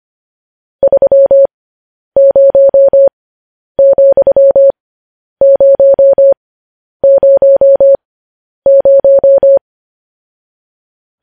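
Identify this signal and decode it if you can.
Morse code, heard as a keyed tone, about 25 wpm, spaced as for 10 wpm, 550 Hz. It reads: 30,000
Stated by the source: Morse code news headlines at 10 wpm.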